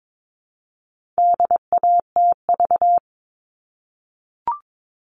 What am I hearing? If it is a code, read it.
Morse code, heard as a keyed tone, about 22 wpm, 700 Hz. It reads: DATV
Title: All Cut Numbers - 4-Digit 22wpm